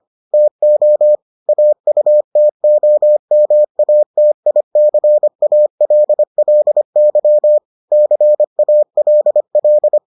Morse code, heard as a keyed tone, about 25 wpm, 600 Hz. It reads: TO AUTOMATICALLY CALL